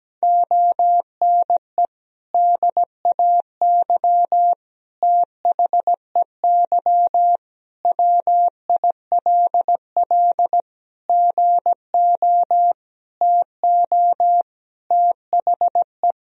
Morse code, heard as a keyed tone, about 17 wpm, 700 Hz. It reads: ONE DAY THEY WILL GO TO THE
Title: Sentences from Top 1000 Words - Review Part 1 - 17wpm